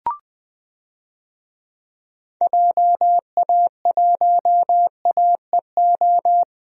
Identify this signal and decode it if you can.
Morse code, heard as a keyed tone, about 20 wpm, 700 Hz. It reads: JA1AEO
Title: DXCC-100 Call Signs - 6 Characters - Repeated 2x - 20wpm